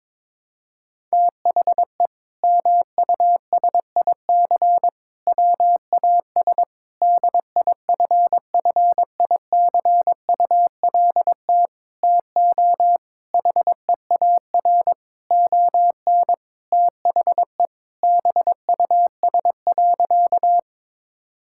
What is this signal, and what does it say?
Morse code, heard as a keyed tone, about 22 wpm, 700 Hz. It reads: THE MUSIC WAS DIFFICULT TO HEAR ON THE BUS.
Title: Sentences from Top 1000 Words - Review Part 6 - 22wpm